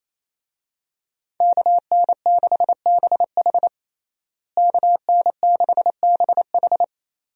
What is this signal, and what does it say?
Morse code, heard as a keyed tone, about 28 wpm, 700 Hz. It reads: KN6BH KN6BH
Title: USA Call Signs - 5 Characters - Repeated 2x - 28wpm